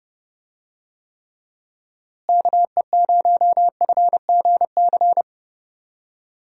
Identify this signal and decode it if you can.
Morse code, heard as a keyed tone, about 30 wpm, 700 Hz. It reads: KE0FGC
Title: USA Call Signs - 6 Characters - Repeated 3x - 30wpm